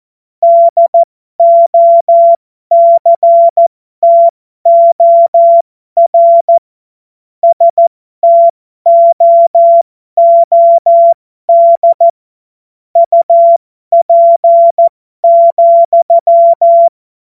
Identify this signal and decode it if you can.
Morse code, heard as a keyed tone, about 14 wpm, spaced as for 12 wpm, 680 Hz. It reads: DOCTOR STOOD UP,